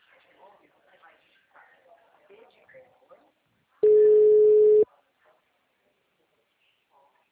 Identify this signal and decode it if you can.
Ringback tone (Europe, 425 Hz), with speech.